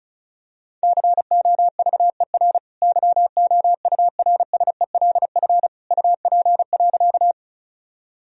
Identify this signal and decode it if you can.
Morse code, heard as a keyed tone, about 36 wpm, 700 Hz. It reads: COVER YOURSELF UP.
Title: Sentences from Top 1000 Words - Review Part 1 - 35wpm